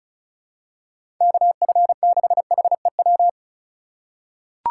Morse code, heard as a keyed tone, about 36 wpm, 700 Hz. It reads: KF6HEW
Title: USA Call Signs - 6 Characters - 3x Character Spacing - 35wpm